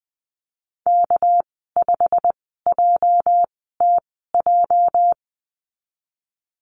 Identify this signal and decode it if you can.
Morse code, heard as a keyed tone, about 20 wpm, spaced as for 15 wpm, 700 Hz. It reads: K5JTJ
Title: 207 Lesson 33g - Letter j - Call Signs 20wpm